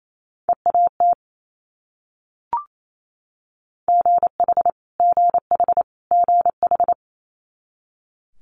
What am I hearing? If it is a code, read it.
Morse code, heard as a keyed tone, about 28 wpm, 700 Hz. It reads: EAT GH GH GH
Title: Multi-letter Phonemes Repeated 3x 28wpm